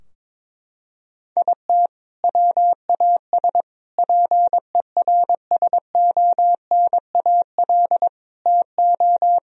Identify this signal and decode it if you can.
Morse code, heard as a keyed tone, about 22 wpm, 700 Hz. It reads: IT WAS PERSONAL TO
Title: Sentences from Top 1000 Words - Review Part 4 - 22wpm